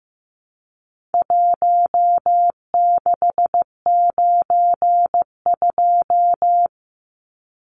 Morse code, heard as a keyed tone, about 15 wpm, 700 Hz. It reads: 1692
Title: Numbers - 4-Digit 15wpm